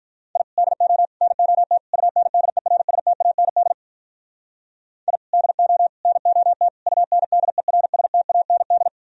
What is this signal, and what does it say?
Morse code, heard as a keyed tone, about 44 wpm, 700 Hz. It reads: I DO NOT UNDERSTAND I DO NOT UNDERSTAND